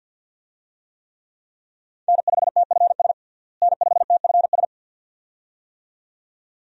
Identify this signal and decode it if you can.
Morse code, heard as a keyed tone, about 46 wpm, 700 Hz. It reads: N5TFS N5TFS
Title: USA Call Signs - 5 Characters - Repeated 2x - 50wpm